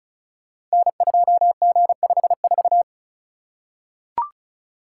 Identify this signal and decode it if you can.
Morse code, heard as a keyed tone, about 36 wpm, 700 Hz. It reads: N2G54
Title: Code Groups - 5 Letter:Numbers - 35wpm